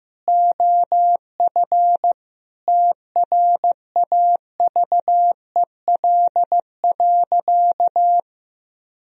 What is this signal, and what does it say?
Morse code, heard as a keyed tone, about 15 wpm, 700 Hz. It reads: OF TRAVEL.
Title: Sentences from Top 1000 Words - Review Part 4 - 15wpm